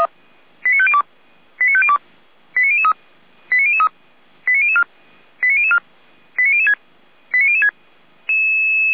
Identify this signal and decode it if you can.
Five-tone selective call (ZVEI), heard as quick runs of five tones, tones of about 70 ms. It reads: 89502 89502 89003 89003 89004 89004 89006 89006